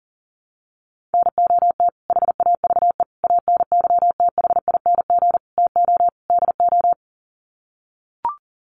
Morse code, heard as a keyed tone, about 40 wpm, 700 Hz. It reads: NOT HAVE ANYTHING TO DO
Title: Sets of 5 Words Encore 40wpm